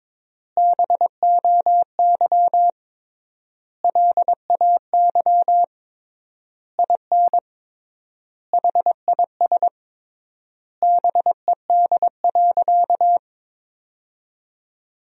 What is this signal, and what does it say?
Morse code, heard as a keyed tone, about 22 wpm, 700 Hz. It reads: BOY LAY IN HIS BED.
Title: 3x Word-Spacing - Sentences from Top 500 Words - 22wpm